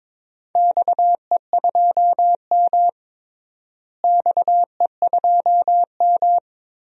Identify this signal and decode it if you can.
Morse code, heard as a keyed tone, about 22 wpm, 700 Hz. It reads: XE2M XE2M
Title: DXCC-100 Call Signs - 4 Characters - Repeated 3x - 22wpm